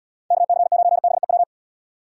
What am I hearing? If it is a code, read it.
Morse code, heard as a keyed tone, about 76 wpm, 700 Hz. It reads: KC9BEW